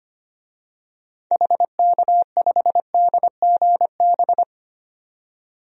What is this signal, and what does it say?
Morse code, heard as a keyed tone, about 25 wpm, 700 Hz. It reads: HK5DGB
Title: DXCC-100 Call Signs - 6 Characters - 25wpm